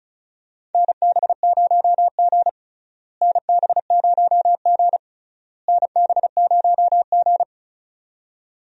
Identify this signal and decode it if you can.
Morse code, heard as a keyed tone, about 34 wpm, 700 Hz. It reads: NB0G NB0G NB0G
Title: USA Call Signs - 4 Characters - Repeated 3x - 35wpm